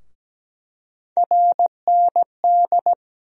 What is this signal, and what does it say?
Morse code, heard as a keyed tone, about 17 wpm, 700 Hz. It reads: RND
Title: USA Call Sign Postfixes - 3 Letters 17wpm